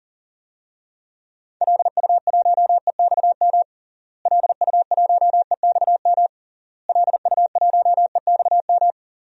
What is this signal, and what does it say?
Morse code, heard as a keyed tone, about 40 wpm, 700 Hz. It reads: LU1EXM LU1EXM LU1EXM
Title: DXCC-100 Call Signs - 6 Characters - Repeated 3x - 40wpm